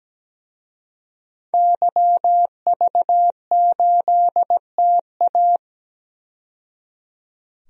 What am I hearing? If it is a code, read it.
Morse code, heard as a keyed tone, about 17 wpm, 700 Hz. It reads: YV8TA
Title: DXCC-100 Call Signs - 5 Characters - 17wpm